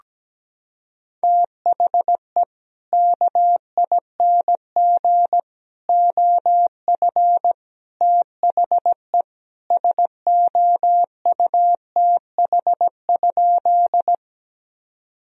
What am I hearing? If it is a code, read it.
Morse code, heard as a keyed tone, about 17 wpm, 700 Hz. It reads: THE KING OF THE SOUTH?
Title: Sentences from Top 1000 Words - Review Part 2 - 17wpm